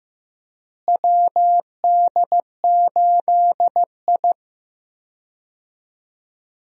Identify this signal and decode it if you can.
Morse code, heard as a keyed tone, about 15 wpm, 700 Hz. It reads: WD8I
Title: USA Call Signs - 4 Characters - Repeated 2x - 15wpm